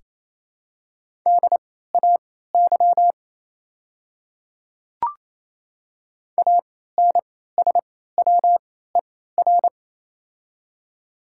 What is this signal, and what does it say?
Morse code, heard as a keyed tone, about 28 wpm, spaced as for 16 wpm, 700 Hz. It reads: DAY ANSWER
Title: 3x Character Spacing - Top 300 Words - 28wpm